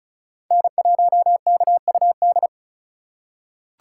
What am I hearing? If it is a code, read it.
Morse code, heard as a keyed tone, about 34 wpm, 700 Hz. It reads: N1KUD
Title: USA Call Signs - 5 Characters - 2x Character Spacing - 35wpm